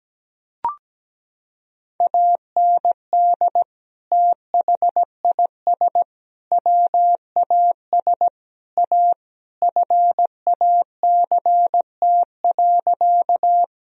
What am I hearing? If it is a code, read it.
Morse code, heard as a keyed tone, about 17 wpm, 700 Hz. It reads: AND THIS WAS A FACT.
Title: Sentences from Top 1000 Words - Review Part 3 - 17wpm